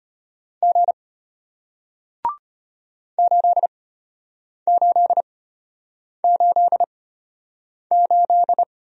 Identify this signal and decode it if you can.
Morse code, heard as a keyed tone, about 33 wpm, spaced as for 8 wpm, 700 Hz. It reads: G 8888